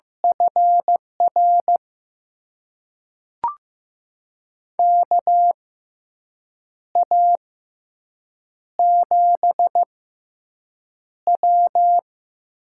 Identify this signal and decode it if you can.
Morse code, heard as a keyed tone, about 16 wpm, spaced as for 6 wpm, 700 Hz. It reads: FR KA7W
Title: USA Call Signs - 6 Characters - 6x Character Spacing - 15wpm